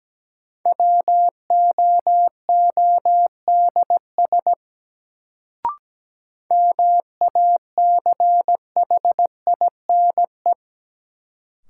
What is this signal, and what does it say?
Morse code, heard as a keyed tone, about 17 wpm, 700 Hz. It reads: WOODS MACHINE
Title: Sentences from Top 1000 Words - Review Part 2 - New Word Drill - 17wpm